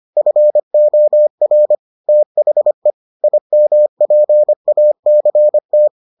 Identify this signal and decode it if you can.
Morse code, heard as a keyed tone, about 25 wpm, 600 Hz. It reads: FOR THE IMPACT